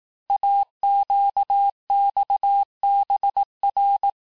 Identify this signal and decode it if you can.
Morse code, heard as a keyed tone, about 18 wpm, 800 Hz. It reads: AQXBR
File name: cours-du-lundi files / cours-1588-texte-chiffre-vitesse-17MPM